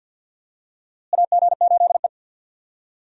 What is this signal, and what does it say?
Morse code, heard as a keyed tone, about 52 wpm, 700 Hz. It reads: AG8E
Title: USA Call Signs - 4 Characters - Repeated 2x - 50wpm